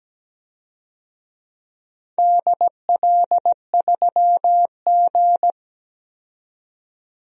Morse code, heard as a keyed tone, about 17 wpm, 700 Hz. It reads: DL3G
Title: DXCC-100 Call Signs - 4 Characters - 17wpm